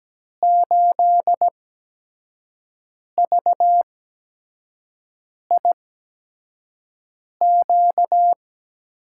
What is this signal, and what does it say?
Morse code, heard as a keyed tone, about 17 wpm, spaced as for 5 wpm, 700 Hz. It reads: 8VIQ
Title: USA Call Signs - 8x Character Spacing - 17wpm